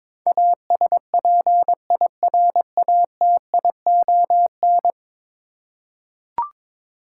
Morse code, heard as a keyed tone, about 22 wpm, 700 Hz. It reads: ASPIRATION
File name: practice / 22wpm / 10 Letter Words - First 3 Letters Given - 22wpm